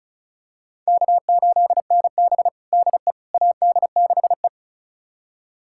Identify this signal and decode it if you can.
Morse code, heard as a keyed tone, about 34 wpm, 700 Hz. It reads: K8NB DE AD6E
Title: USA Callsign DE Callsign - 4 Characters - 35wpm